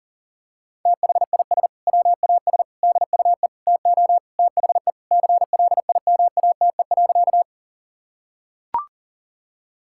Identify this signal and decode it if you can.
Morse code, heard as a keyed tone, about 37 wpm, 700 Hz. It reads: THIS WAS DUE TO THE CLIMATE.
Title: Sentences from Top 1600 Words 40wpm